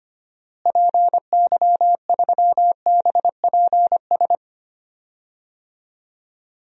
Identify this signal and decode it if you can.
Morse code, heard as a keyed tone, about 25 wpm, 700 Hz. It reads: PY3BPS
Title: DXCC-100 Call Signs - 6 Characters - Repeated 2x - 25wpm